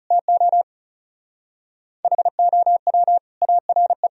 Morse code, heard as a keyed tone, about 37 wpm, 700 Hz. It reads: TO HOW ARE
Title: Speed-Racing - Sets of 5 Words 30wpm